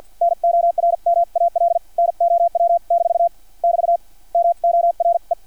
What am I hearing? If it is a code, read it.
Morse code, heard as a keyed tone, about 50 wpm, spaced as for 43 wpm, 680 Hz. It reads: NORMAL NOW- <BT> MORE
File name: archived / CRSnet